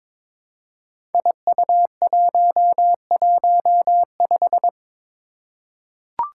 Morse code, heard as a keyed tone, about 23 wpm, 700 Hz. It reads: IU115
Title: Five Challenging Letter-Number Combinations 22wpm